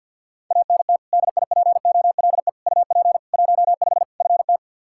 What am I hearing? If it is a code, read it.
Morse code, heard as a keyed tone, about 52 wpm, 700 Hz. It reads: ANT DIPOLE UP 15 FT
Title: Practice QSO - Exchange 2 - Info - 50wpm